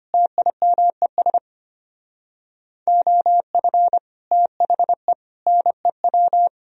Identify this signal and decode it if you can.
Morse code, heard as a keyed tone, about 31 wpm, 700 Hz. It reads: TIMES OF THE NEW
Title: Speed-Racing - Sets of 5 Words 25wpm